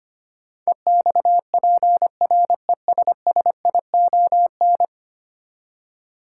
Morse code, heard as a keyed tone, about 25 wpm, 700 Hz. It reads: EXPRESSION
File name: Ten Letter Words - 25wpm